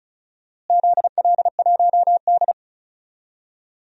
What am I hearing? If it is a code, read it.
Morse code, heard as a keyed tone, about 34 wpm, 700 Hz. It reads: ZL1D
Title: DXCC-100 Call Signs - 4 Characters - Repeated 3x - 35wpm